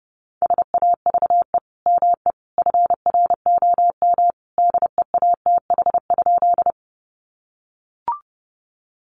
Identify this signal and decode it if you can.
Morse code, heard as a keyed tone, about 30 wpm, 700 Hz. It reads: SAVE ME FROM DEATH?
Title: Sentences from Top 1100 Words 30wpm